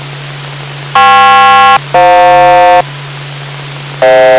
Aircraft SELCAL, heard as two pairs of tones. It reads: KP-FJ